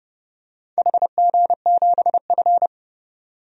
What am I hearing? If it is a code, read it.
Morse code, heard as a keyed tone, about 28 wpm, 700 Hz. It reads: HG7F
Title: DXCC-100 Call Signs - 4 Characters - 30wpm